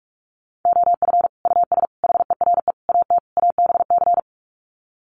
Morse code, heard as a keyed tone, about 44 wpm, 700 Hz. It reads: OF US HERE AT ABC
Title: Sets of 5 Words 45wpm